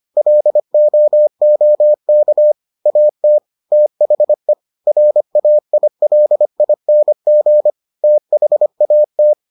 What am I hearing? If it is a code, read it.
Morse code, heard as a keyed tone, about 25 wpm, 600 Hz. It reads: LOOK AT THE RAILING THAT